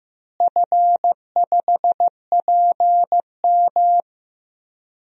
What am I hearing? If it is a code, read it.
Morse code, heard as a keyed tone, about 15 wpm, 700 Hz. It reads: F5PM
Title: DXCC-100 Call Signs - 4 Characters - Repeated 3x - 15wpm